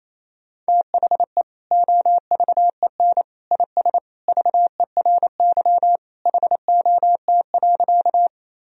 Morse code, heard as a keyed tone, about 28 wpm, 700 Hz. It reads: THE OVEN IS VERY HOT.
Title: Sentences from Top 1000 Words - Review Part 4 - 28wpm